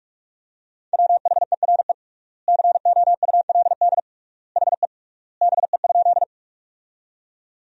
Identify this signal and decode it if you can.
Morse code, heard as a keyed tone, about 46 wpm, 700 Hz. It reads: WHERE COULD HE BE?
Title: 3x Word-Spacing - Sentences from Top 200 Words - 45wpm